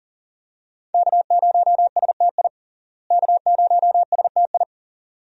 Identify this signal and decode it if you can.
Morse code, heard as a keyed tone, about 37 wpm, 700 Hz. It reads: K0STI K0STI
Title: USA Call Signs - 5 Characters - Repeated 2x - 40wpm